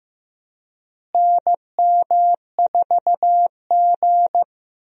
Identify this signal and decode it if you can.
Morse code, heard as a keyed tone, about 15 wpm, 700 Hz. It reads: NM4G
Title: USA Call Signs - 4 Characters - Repeated 2x - 15wpm